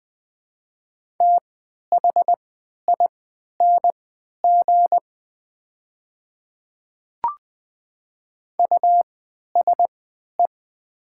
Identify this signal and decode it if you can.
Morse code, heard as a keyed tone, about 20 wpm, spaced as for 11 wpm, 700 Hz. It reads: THING USE
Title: 3x Character Spacing - Top 100 Words - 20wpm